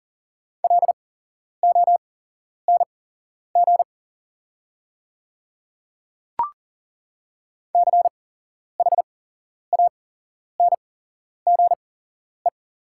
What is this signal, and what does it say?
Morse code, heard as a keyed tone, about 40 wpm, spaced as for 11 wpm, 700 Hz. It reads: LONG CHANGE
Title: 8x Character Spacing - Top 200 Words - 40wpm